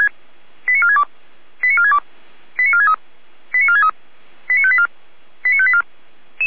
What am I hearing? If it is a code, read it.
Five-tone selective call (ZVEI), heard as quick runs of five tones, tones of about 70 ms. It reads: 89462 89462 89463 89463 89584 89584
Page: ZVEI